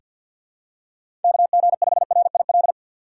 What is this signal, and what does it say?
Morse code, heard as a keyed tone, about 47 wpm, 700 Hz. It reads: KG5RIL
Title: USA Call Signs - 6 Characters - 4x Character Spacing - 50wpm